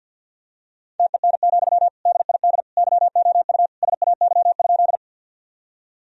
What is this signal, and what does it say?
Morse code, heard as a keyed tone, about 51 wpm, 700 Hz. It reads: TEN, DID YOU SAY?
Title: Sentences from Top 1000 Words - Review Part 2 - 50wpm